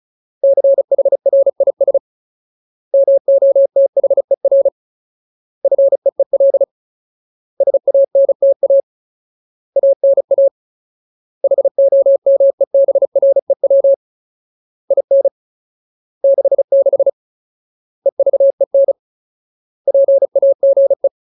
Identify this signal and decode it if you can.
Morse code, heard as a keyed tone, about 34 wpm, 550 Hz. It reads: CHRIS MOTHER FEEL SANTA ANA HOMEBREW IN 66 EVEN PAGE
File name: Ultimate Word List / 35WPM Ultimate